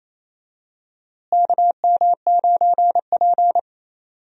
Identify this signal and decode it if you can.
Morse code, heard as a keyed tone, about 28 wpm, 700 Hz. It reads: KM9P
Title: USA Call Signs - 4 Characters - Repeated 2x - 28wpm